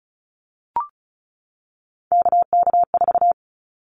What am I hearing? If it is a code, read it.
Morse code, heard as a keyed tone, about 34 wpm, 700 Hz. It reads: KK4
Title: Callsign Prefixes for Top 25 DXCC - 35wpm